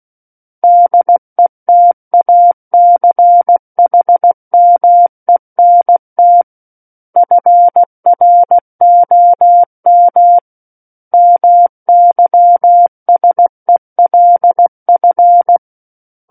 Morse code, heard as a keyed tone, about 16 wpm, 700 Hz. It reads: DETACHMENT FROM MYSELF
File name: Book One / 07-Chapter 7 - 16 WPM